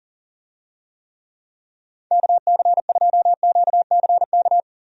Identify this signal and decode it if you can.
Morse code, heard as a keyed tone, about 40 wpm, 700 Hz. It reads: KC2QCK
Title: USA Call Signs - 6 Characters - 40wpm